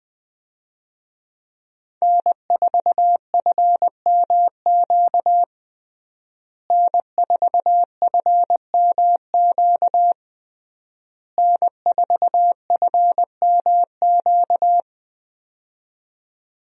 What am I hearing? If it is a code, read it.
Morse code, heard as a keyed tone, about 20 wpm, 700 Hz. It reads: N4FMQ N4FMQ N4FMQ
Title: USA Call Signs - 5 Characters - Repeated 3x - 20wpm